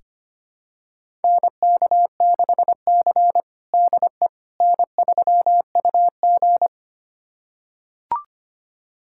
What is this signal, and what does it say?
Morse code, heard as a keyed tone, about 25 wpm, 700 Hz. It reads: NK6C DE N3UG
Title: USA Callsign DE Callsign - 4 Characters - 25wpm